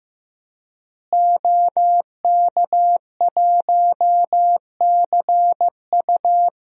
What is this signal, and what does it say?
Morse code, heard as a keyed tone, about 15 wpm, 700 Hz. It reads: OK1CU
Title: DXCC-100 Call Signs - 5 Characters - Repeated 2x - 15wpm